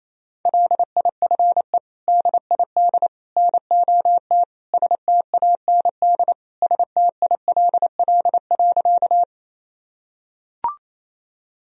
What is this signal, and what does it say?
Morse code, heard as a keyed tone, about 28 wpm, 700 Hz. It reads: LIFE DID NOT STAND STILL.
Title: Sentences from Top 1000 Words - Review Part 1 - 28wpm